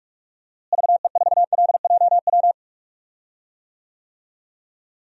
Morse code, heard as a keyed tone, about 46 wpm, 700 Hz. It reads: VE4LJW
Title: DXCC-100 Call Signs - 6 Characters - Repeated 2x - 45wpm